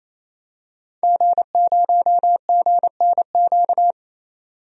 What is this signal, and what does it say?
Morse code, heard as a keyed tone, about 28 wpm, 700 Hz. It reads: G0GNQ
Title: DXCC-100 Call Signs - 5 Characters - 28wpm